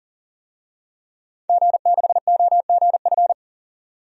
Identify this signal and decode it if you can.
Morse code, heard as a keyed tone, about 40 wpm, 700 Hz. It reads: G6OGF
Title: DXCC-100 Call Signs - 5 Characters - 40wpm